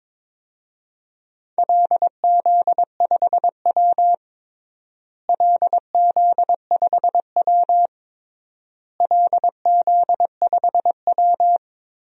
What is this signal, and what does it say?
Morse code, heard as a keyed tone, about 22 wpm, 700 Hz. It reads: LZ5W LZ5W LZ5W
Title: DXCC-100 Call Signs - 4 Characters - Repeated 3x - 22wpm